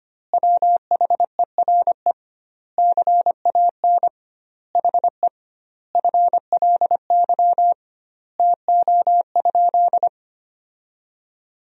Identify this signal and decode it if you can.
Morse code, heard as a keyed tone, about 25 wpm, 700 Hz. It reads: WHERE CAN HE FLY TO?
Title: 2x Word-Spacing - Sentences from Top 500 Words - 25wpm